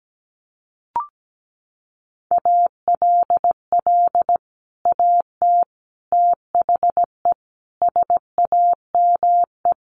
Morse code, heard as a keyed tone, about 17 wpm, 700 Hz. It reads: ALL AT THE SAME
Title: Sets of 5 Words Encore2 17wpm